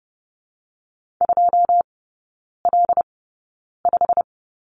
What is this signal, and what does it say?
Morse code, heard as a keyed tone, about 30 wpm, spaced as for 9 wpm, 700 Hz. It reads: 2L5
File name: Lesson 17e - Letter u - Three Characters 30wpm